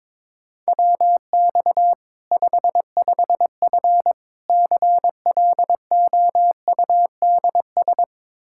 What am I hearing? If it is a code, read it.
Morse code, heard as a keyed tone, about 22 wpm, 700 Hz. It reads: WX 55F CLOUDS
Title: Practice QSO - Exchange 2 - Info - 22wpm